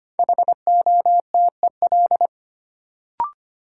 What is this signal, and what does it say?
Morse code, heard as a keyed tone, about 25 wpm, 700 Hz. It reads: HOTEL
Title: Sentences from Top 1600 Words - New Word Drill - 25wpm